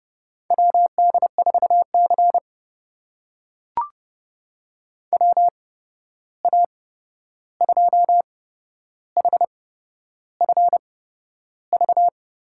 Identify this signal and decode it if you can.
Morse code, heard as a keyed tone, about 30 wpm, spaced as for 11 wpm, 700 Hz. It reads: WD4C WA2HFV
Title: USA Call Signs - 8x Character Spacing - 30wpm